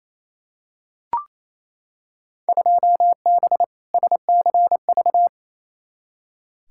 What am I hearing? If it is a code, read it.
Morse code, heard as a keyed tone, about 28 wpm, 700 Hz. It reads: 2B SCV
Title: Field Day Exchanges Encore - 28wpm